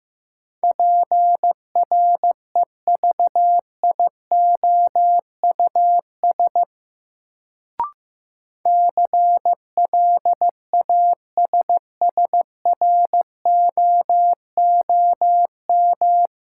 Morse code, heard as a keyed tone, about 15 wpm, 700 Hz. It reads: PREVIOUS CLASSROOM
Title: Sentences from Top 1500 Words - New Word Drill Alt - 15wpm